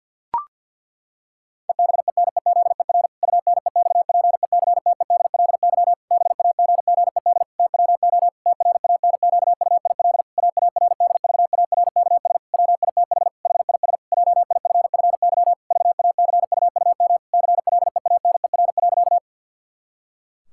Morse code, heard as a keyed tone, about 48 wpm, 700 Hz. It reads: EBENEZER UNEXPECTEDLY BAGGED TWO TRANQUIL AARDVARKS WITH HIS JIFFY VACUUM CLEANER.
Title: Pangram Challenge v2 - 50wpm